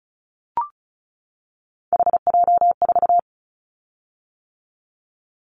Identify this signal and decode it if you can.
Morse code, heard as a keyed tone, about 34 wpm, 700 Hz. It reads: HJ4
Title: Callsign Prefixes for Top 100 DXCC - 35wpm